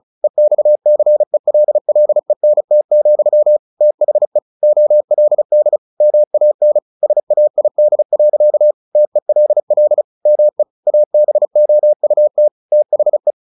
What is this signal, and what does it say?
Morse code, heard as a keyed tone, about 34 wpm, 600 Hz. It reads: EXCELLENT, THE OLD MAN SAID. TELL ME ABOUT THE
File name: the old man and the sea / the old man and the sea 1 35wpm